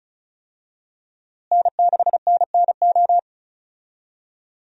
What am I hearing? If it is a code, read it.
Morse code, heard as a keyed tone, about 36 wpm, 700 Hz. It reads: N6NNO
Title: USA Call Signs - 5 Characters - Repeated 3x - 35wpm